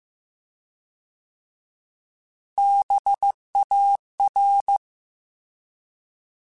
Morse code, heard as a keyed tone, about 15 wpm, 800 Hz. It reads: BAR